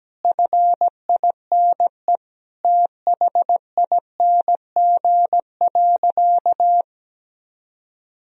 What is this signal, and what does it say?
Morse code, heard as a keyed tone, about 17 wpm, 700 Hz. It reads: FINE THING.
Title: Sentences from Top 500 Words and Abbreviations 17wpm